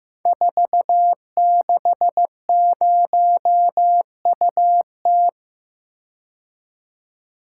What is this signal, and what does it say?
Morse code, heard as a keyed tone, about 15 wpm, 700 Hz. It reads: 460UT
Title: Code Groups - 5 Letter:Numbers - 15wpm